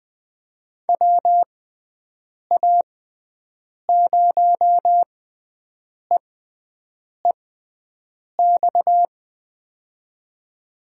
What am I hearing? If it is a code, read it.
Morse code, heard as a keyed tone, about 20 wpm, spaced as for 7 wpm, 700 Hz. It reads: WA0EEX
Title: USA Call Signs - 6 Characters - 6x Character Spacing - 20wpm